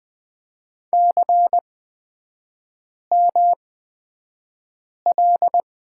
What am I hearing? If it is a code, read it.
Morse code, heard as a keyed tone, about 20 wpm, spaced as for 5 wpm, 700 Hz. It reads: CML
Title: 250 Lesson 40e - BK - Three Characters 20wpm